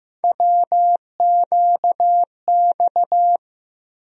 Morse code, heard as a keyed tone, about 15 wpm, 700 Hz. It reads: WQX
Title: USA Call Sign Postfixes - 3 Letters 15wpm